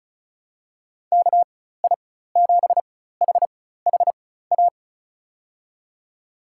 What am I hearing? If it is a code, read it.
Morse code, heard as a keyed tone, about 36 wpm, spaced as for 16 wpm, 700 Hz. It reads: KI7HHA
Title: USA Call Signs - 6 Characters - 4x Character Spacing - 35wpm